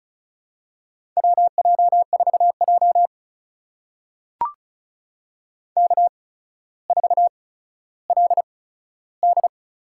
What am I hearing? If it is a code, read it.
Morse code, heard as a keyed tone, about 36 wpm, spaced as for 15 wpm, 700 Hz. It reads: WJ4J K4LD
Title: USA Call Signs - 4 Characters - 8x Character Spacing - 35wpm